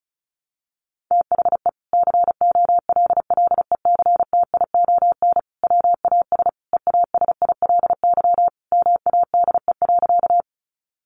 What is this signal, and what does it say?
Morse code, heard as a keyed tone, about 34 wpm, 700 Hz. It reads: THE COLLECTION WAS EASILY MADE.
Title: Sentences from Top 1200 Words 35wpm